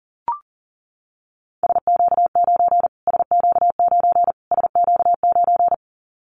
Morse code, heard as a keyed tone, about 40 wpm, 700 Hz. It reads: SQ9 SQ9 SQ9
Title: Callsign Prefixes for Top 25 DXCC - Repeated 3x - 40wpm